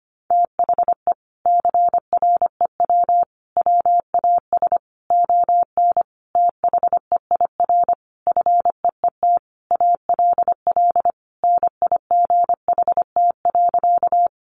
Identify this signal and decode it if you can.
Morse code, heard as a keyed tone, about 25 wpm, 700 Hz. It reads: THE CREW WAS ON THEIR FEET ALL NIGHT.
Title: Sentences from Top 1500 Words 25wpm